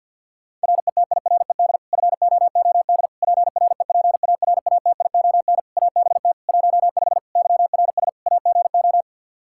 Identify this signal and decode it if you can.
Morse code, heard as a keyed tone, about 48 wpm, 700 Hz. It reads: RETIRED FOOD PREPARATION ABT 15 YRS AGO
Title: K1USN SST QSO Practice 50wpm